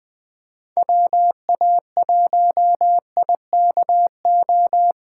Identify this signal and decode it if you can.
Morse code, heard as a keyed tone, about 20 wpm, 700 Hz. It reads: WA1IKO
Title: Speed Racing - USA 2x3 Call Signs 17wpm